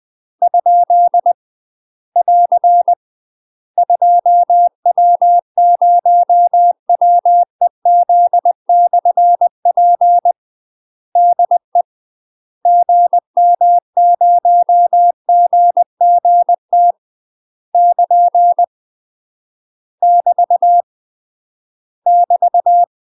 Morse code, heard as a keyed tone, about 20 wpm, 700 Hz. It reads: ? <AR> 2W0WEZ/P DE GM0GGT <KN> <BT> <BT>
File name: UK MASTER QSOs TRAINING / 20WPM / UK MASTER QSOs 0011